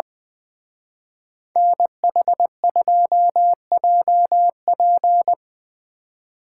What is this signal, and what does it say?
Morse code, heard as a keyed tone, about 22 wpm, 700 Hz. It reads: NH2JP
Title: Speed Racing - USA 2x2 Call Signs 20wpm